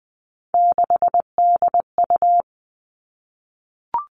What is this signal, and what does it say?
Morse code, heard as a keyed tone, about 20 wpm, 700 Hz. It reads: 6DU
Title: Three Challenging Letter-Number Combinations 20wpm